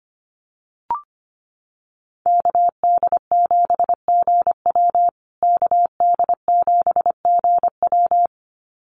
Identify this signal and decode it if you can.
Morse code, heard as a keyed tone, about 25 wpm, 700 Hz. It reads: KD7GW KD7GW
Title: USA Call Signs Repeated 2x 25wpm